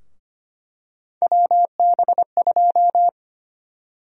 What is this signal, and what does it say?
Morse code, heard as a keyed tone, about 25 wpm, 700 Hz. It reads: WB2
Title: USA Call Sign Prefixes - Repeated 2x - 25wpm